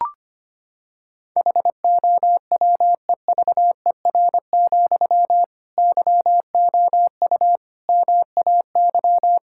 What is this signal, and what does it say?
Morse code, heard as a keyed tone, about 25 wpm, 700 Hz. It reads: HOWEVER, YOU MAY
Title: Sentences from Top 1600 Words 25wpm